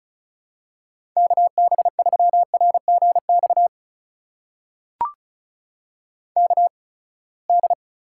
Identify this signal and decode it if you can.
Morse code, heard as a keyed tone, about 34 wpm, spaced as for 24 wpm, 700 Hz. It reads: KB3RGX KD
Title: USA Call Signs - 6 Characters - 8x Character Spacing - 35wpm